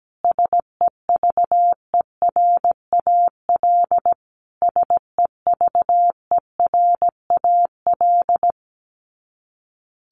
Morse code, heard as a keyed tone, about 17 wpm, 700 Hz. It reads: SEVERAL SEVERAL
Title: Top 500 Words Repeated 17wpm